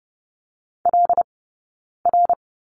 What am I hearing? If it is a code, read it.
Morse code, heard as a keyed tone, about 30 wpm, spaced as for 9 wpm, 690 Hz. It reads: LR
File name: course / 30:10wpm / 086 Lesson 15d - Number 5 - Two Characters 30wpm